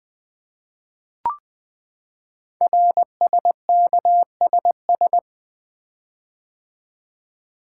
Morse code, heard as a keyed tone, about 20 wpm, 700 Hz. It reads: RSKSS
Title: Code Groups - 5 Letters - 20wpm